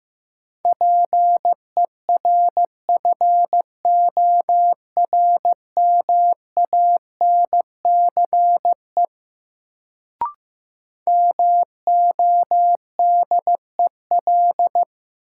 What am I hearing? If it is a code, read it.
Morse code, heard as a keyed tone, about 15 wpm, 700 Hz. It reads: PERFORMANCE MODEL
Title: Sentences from Top 1000 Words - New Word Drill - 15wpm